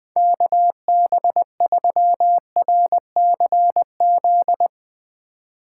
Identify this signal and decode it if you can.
Morse code, heard as a keyed tone, about 20 wpm, 700 Hz. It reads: KB3RCZ
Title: Speed Racing - USA 2x3 Call Signs 20wpm